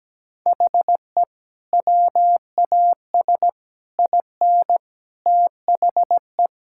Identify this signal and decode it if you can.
Morse code, heard as a keyed tone, about 17 wpm, 700 Hz. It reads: HE WAS IN THE
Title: Sentences from Top 1600 Words 17wpm